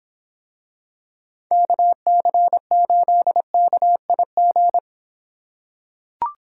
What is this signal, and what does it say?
Morse code, heard as a keyed tone, about 25 wpm, 700 Hz. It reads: KC8KIG